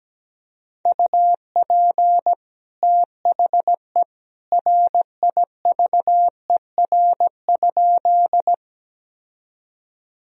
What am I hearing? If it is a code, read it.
Morse code, heard as a keyed tone, about 17 wpm, 700 Hz. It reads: UP THE RIVER?
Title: Sentences from Top 1000 Words - Review Part 1 - 17wpm